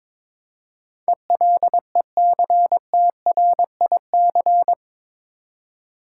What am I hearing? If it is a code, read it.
Morse code, heard as a keyed tone, about 21 wpm, 700 Hz. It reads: ELECTRIC
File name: practice / 22wpm / Eight Letter Words - 22wpm